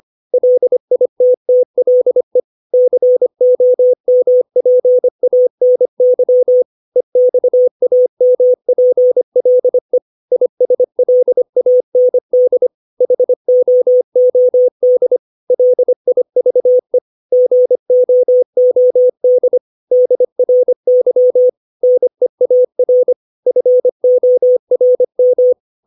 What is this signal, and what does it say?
Morse code, heard as a keyed tone, about 25 wpm, 500 Hz. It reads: LITTLE COMPANY EXAMPLE ISLAND HOOD LIVE GOOD DRY NEAR FORM
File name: 632 words / 632 Words 25WPM